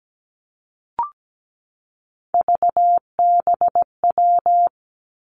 Morse code, heard as a keyed tone, about 17 wpm, 700 Hz. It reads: VBW
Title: Three Challenging Letter-Number Combinations 17wpm